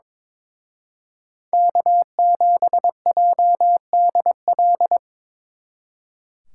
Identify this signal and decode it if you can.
Morse code, heard as a keyed tone, about 25 wpm, 700 Hz. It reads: K7JDL
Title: Speed Racing - USA 1x3 Call Signs 22wpm